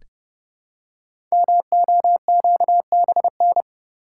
Morse code, heard as a keyed tone, about 29 wpm, 700 Hz. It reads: MOQBN
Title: Code Groups - 5 Letters - 30wpm